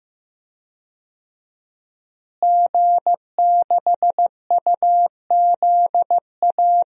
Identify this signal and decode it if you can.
Morse code, heard as a keyed tone, about 16 wpm, 700 Hz. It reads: G6UZA
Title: DXCC-100 Call Signs - 5 Characters - Repeated 2x - 15wpm